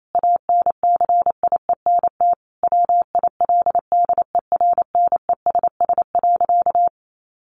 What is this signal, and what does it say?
Morse code, heard as a keyed tone, about 28 wpm, 700 Hz. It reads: ANCIENT WILDERNESS.
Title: A Sound Of Thunder 28wpm